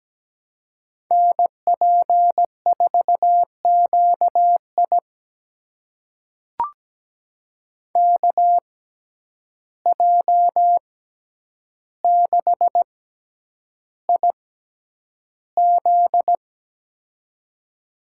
Morse code, heard as a keyed tone, about 17 wpm, spaced as for 9 wpm, 700 Hz. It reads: NP4QI KJ6IZ
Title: USA Call Signs - 5 Characters - 6x Character Spacing - 17wpm